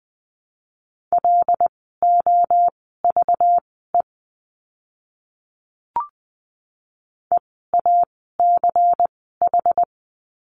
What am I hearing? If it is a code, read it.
Morse code, heard as a keyed tone, about 20 wpm, spaced as for 15 wpm, 700 Hz. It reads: LOVE EACH
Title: 192 Lesson 31f - Letter v - Words 20wpm